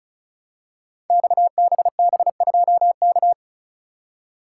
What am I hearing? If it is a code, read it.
Morse code, heard as a keyed tone, about 36 wpm, 700 Hz. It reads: XBB2K
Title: Code Groups - 5 Letter:Numbers - 35wpm